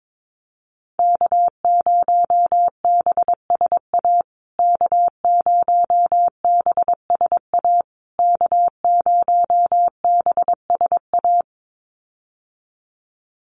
Morse code, heard as a keyed tone, about 22 wpm, 700 Hz. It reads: K0BSA K0BSA K0BSA
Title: USA Call Signs Repeated 3x 22wpm